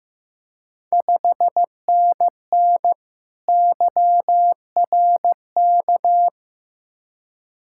Morse code, heard as a keyed tone, about 15 wpm, 700 Hz. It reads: 5NN YRK
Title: Ontario QSO Party Exchange 15wpm